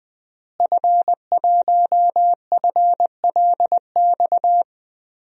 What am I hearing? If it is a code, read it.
Morse code, heard as a keyed tone, about 20 wpm, 700 Hz. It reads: F1FLX
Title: DXCC-100 Call Signs - 5 Characters - Repeated 3x - 20wpm